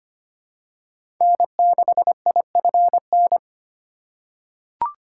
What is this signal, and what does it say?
Morse code, heard as a keyed tone, about 25 wpm, 700 Hz. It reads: N6IFN
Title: USA Call Signs - 5 Characters - 3x Character Spacing - 25wpm